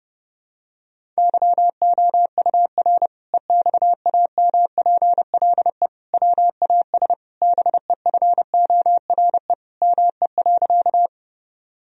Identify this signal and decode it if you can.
Morse code, heard as a keyed tone, about 32 wpm, 700 Hz. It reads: YOUR EXAMPLE WAS BEFORE ME.
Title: Sentences from Top 1000 Words - Review Part 1 - 30wpm